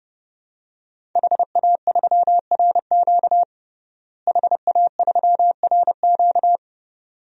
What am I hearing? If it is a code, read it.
Morse code, heard as a keyed tone, about 29 wpm, 700 Hz. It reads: HA3RQ HA3RQ
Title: DXCC-100 Call Signs - 5 Characters - Repeated 2x - 30wpm